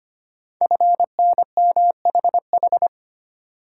Morse code, heard as a keyed tone, about 25 wpm, 700 Hz. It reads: FNMHH
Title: Code Groups - 5 Letters - 25wpm